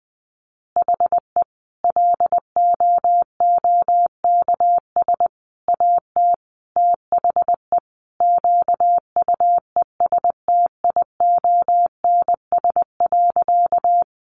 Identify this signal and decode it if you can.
Morse code, heard as a keyed tone, about 20 wpm, 700 Hz. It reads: HE LOOKS AT THE QUESTIONS.
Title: Additional Easy Phrases 20wpm